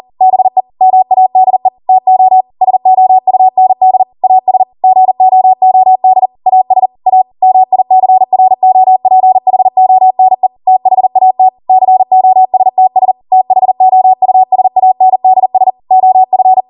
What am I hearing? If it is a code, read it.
Morse code, heard as a keyed tone, about 40 wpm, 750 Hz. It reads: BE MADE TO SOUND AS GOOD AS A MICROPHONE THAT COSTS THOUSANDS OF